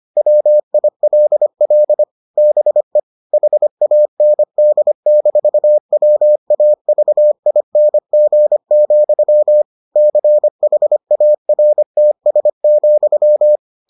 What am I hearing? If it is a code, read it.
Morse code, heard as a keyed tone, about 25 wpm, 600 Hz. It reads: WILL BE HAND-WAVING, CHARTS,